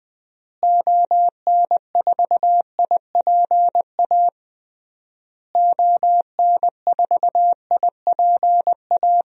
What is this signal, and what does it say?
Morse code, heard as a keyed tone, about 20 wpm, 700 Hz. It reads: ON4IPA ON4IPA
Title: DXCC-100 Call Signs - 6 Characters - Repeated 3x - 20wpm